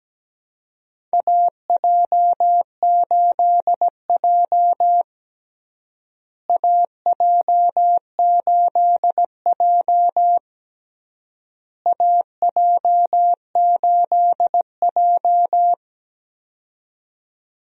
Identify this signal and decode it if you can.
Morse code, heard as a keyed tone, about 17 wpm, 700 Hz. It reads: AJ8J AJ8J AJ8J
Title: USA Call Signs - 4 Characters - Repeated 3x - 17wpm